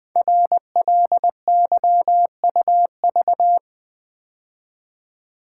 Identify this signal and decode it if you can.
Morse code, heard as a keyed tone, about 20 wpm, 700 Hz. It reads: RLYUV
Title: Code Groups - 5 Letters - 20wpm